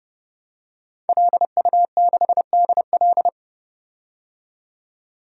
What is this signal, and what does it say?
Morse code, heard as a keyed tone, about 29 wpm, 700 Hz. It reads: LU6DL
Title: DXCC-100 Call Signs - 5 Characters - 30wpm